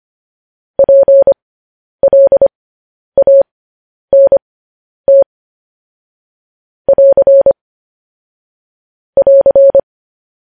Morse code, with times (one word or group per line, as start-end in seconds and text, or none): 0.8-5.2 PLANT
6.9-7.5 <AR>
9.2-9.8 <AR>